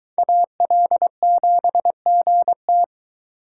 Morse code, heard as a keyed tone, about 24 wpm, 700 Hz. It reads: AL7GT